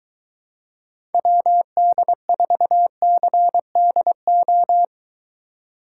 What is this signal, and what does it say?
Morse code, heard as a keyed tone, about 22 wpm, 700 Hz. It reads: WD4CDO